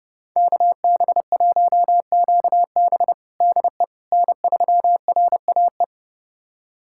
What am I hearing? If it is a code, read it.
Morse code, heard as a keyed tone, about 30 wpm, 700 Hz. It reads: KB1QB DE N3RAE
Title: USA Callsign DE Callsign - 5 Characters - 30wpm